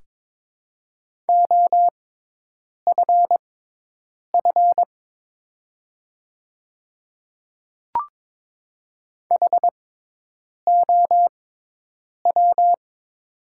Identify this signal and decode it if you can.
Morse code, heard as a keyed tone, about 22 wpm, spaced as for 8 wpm, 700 Hz. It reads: OFF HOW
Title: 6x Character Spacing - Top 300 Words - 22wpm